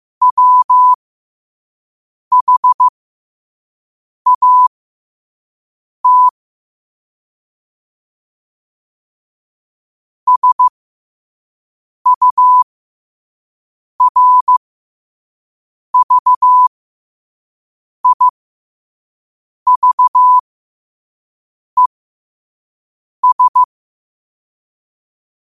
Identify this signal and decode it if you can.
Morse code, heard as a keyed tone, about 15 wpm, spaced as for 5 wpm, 1000 Hz. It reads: WHAT SURVIVES
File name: qotdfiles / 5wpm